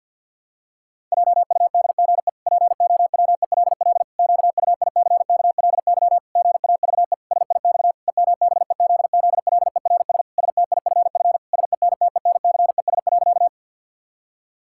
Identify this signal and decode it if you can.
Morse code, heard as a keyed tone, about 49 wpm, 700 Hz. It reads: JUDGE POWELL QUICKLY GAVE SIX EMBEZZLERS STIFF SENTENCES.
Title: Pangram Challenge v2 - 50wpm